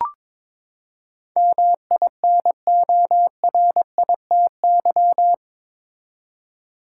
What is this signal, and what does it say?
Morse code, heard as a keyed tone, about 22 wpm, 700 Hz. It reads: MINORITY
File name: practice / 22wpm / Eight Letter Words - 22wpm